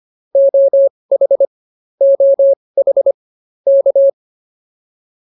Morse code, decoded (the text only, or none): OH OH K